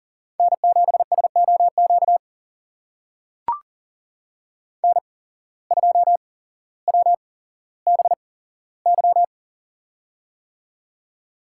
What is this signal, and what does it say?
Morse code, heard as a keyed tone, about 38 wpm, spaced as for 17 wpm, 700 Hz. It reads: N7SOQ N2WBY
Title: USA Call Signs - 5 Characters - 8x Character Spacing - 40wpm